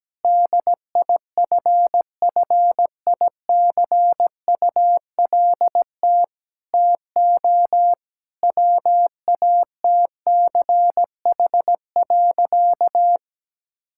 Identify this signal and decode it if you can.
Morse code, heard as a keyed tone, about 17 wpm, 700 Hz. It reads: DIFFICULT TO WATCH.
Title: Sentences from Top 1000 Words - Review Part 5 - 17wpm